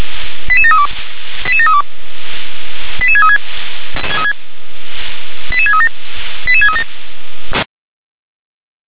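Five-tone selective call (ZVEI), heard as quick runs of five tones, tones of about 70 ms. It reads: 88732 88732 88636 88637 88637
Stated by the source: ZVEI five-tone selective call 88513 received on an SDR.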